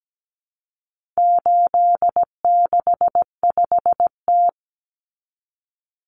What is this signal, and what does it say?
Morse code, heard as a keyed tone, about 17 wpm, 700 Hz. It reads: 865T
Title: Common Cut Numbers - 4-Digit 17wpm